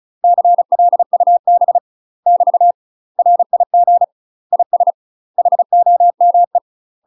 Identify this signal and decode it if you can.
Morse code, heard as a keyed tone, about 36 wpm, 700 Hz. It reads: CLUB <BT> RIG IS HOME
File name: UK MASTER QSOs TRAINING / 35WPM / UK MASTER QSOs 0041